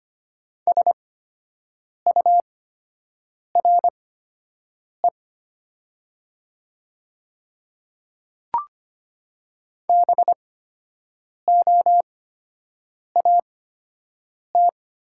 Morse code, heard as a keyed tone, about 25 wpm, spaced as for 7 wpm, 700 Hz. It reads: SURE BOAT
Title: 8x Character Spacing - Top 500 Words - 25wpm